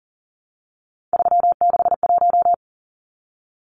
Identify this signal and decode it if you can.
Morse code, heard as a keyed tone, about 40 wpm, 700 Hz. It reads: 361
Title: Numbers - 3-Digit 40wpm